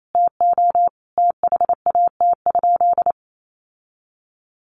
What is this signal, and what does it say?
Morse code, heard as a keyed tone, about 28 wpm, 700 Hz. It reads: TO THAT?
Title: Sentences from Top 1100 Words 28wpm